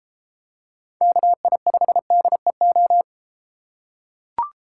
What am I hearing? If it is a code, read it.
Morse code, heard as a keyed tone, about 32 wpm, 700 Hz. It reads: KI5DEO